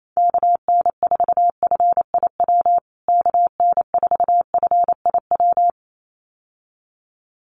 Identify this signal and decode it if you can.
Morse code, heard as a keyed tone, about 28 wpm, 700 Hz. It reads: KN4FIW KN4FIW
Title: USA Call Signs Repeated 2x 28wpm